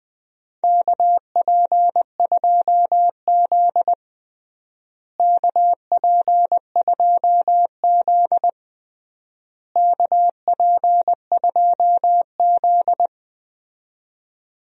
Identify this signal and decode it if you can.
Morse code, heard as a keyed tone, about 20 wpm, 700 Hz. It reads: KP2Z KP2Z KP2Z
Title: DXCC-100 Call Signs - 4 Characters - Repeated 3x - 20wpm